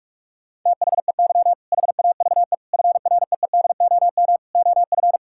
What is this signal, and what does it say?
Morse code, heard as a keyed tone, about 46 wpm, 700 Hz. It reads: THEY HAVE FREEDOM OF
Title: Sentences from Top 1000 Words - Review Part 5 - 45wpm